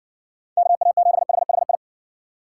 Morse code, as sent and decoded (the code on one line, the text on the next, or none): -.- .- --... .... .... ..
KA7HHI